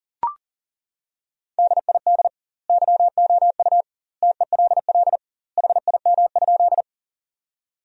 Morse code, heard as a keyed tone, about 40 wpm, 700 Hz. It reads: DID YOU TELL HIM?
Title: 2x Word-Spacing - Sentences from Top 200 Words - 40wpm